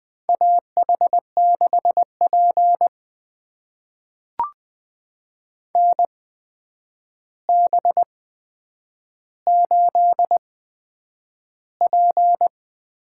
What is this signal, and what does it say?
Morse code, heard as a keyed tone, about 20 wpm, spaced as for 9 wpm, 700 Hz. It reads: AH6P NB8P
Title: USA Call Signs - 4 Characters - 8x Character Spacing - 20wpm